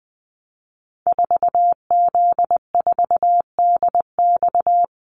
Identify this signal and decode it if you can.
Morse code, heard as a keyed tone, about 20 wpm, 700 Hz. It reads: 4Z4DX
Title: Call Signs Repeated 3x - 20wpm